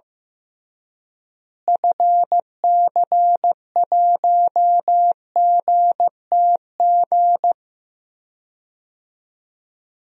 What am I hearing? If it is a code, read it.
Morse code, heard as a keyed tone, about 15 wpm, 700 Hz. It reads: FC1GTG
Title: DXCC-100 Call Signs - 6 Characters - Repeated 3x - 15wpm